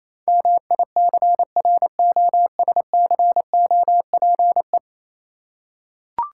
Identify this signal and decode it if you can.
Morse code, heard as a keyed tone, about 27 wpm, 700 Hz. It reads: MICROSCOPE
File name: Easy Words - Speed-Racing with Hints - 22wpm